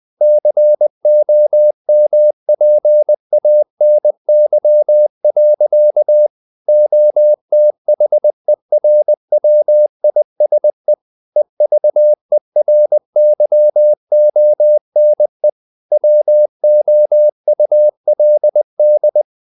Morse code, heard as a keyed tone, about 20 wpm, 600 Hz. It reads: COMPANY. OTHERWISE EVERYONE WOULD
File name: the old man and the sea part 2 20wpm